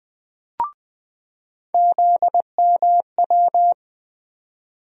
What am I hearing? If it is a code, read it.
Morse code, heard as a keyed tone, about 20 wpm, 700 Hz. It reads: ZMW
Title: USA Call Sign Postfixes - 3 Letters 20wpm